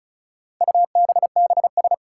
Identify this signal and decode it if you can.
Morse code, heard as a keyed tone, about 34 wpm, 700 Hz. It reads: UBBS
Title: Four Challenging Letter-Number Combinations 35wpm